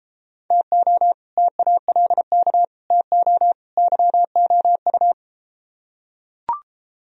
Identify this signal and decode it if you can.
Morse code, heard as a keyed tone, about 32 wpm, 700 Hz. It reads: TO TALK TO YOU